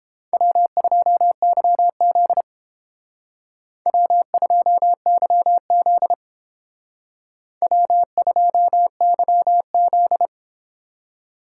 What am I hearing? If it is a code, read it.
Morse code, heard as a keyed tone, about 31 wpm, 700 Hz. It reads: W2YZ W2YZ W2YZ